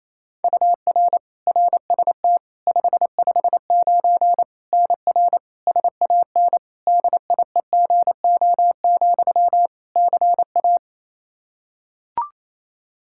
Morse code, heard as a keyed tone, about 28 wpm, 700 Hz. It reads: UR RST 559 NR SAN DIEGO, CA
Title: Practice QSO - Exchange 1 - Beginning - 28wpm